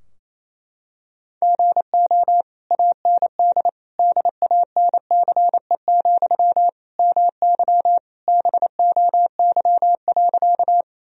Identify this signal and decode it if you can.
Morse code, heard as a keyed tone, about 28 wpm, 700 Hz. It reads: GO AND DANCE, MY BOY.
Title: Sentences from Top 1000 Words - Review Part 4 - 28wpm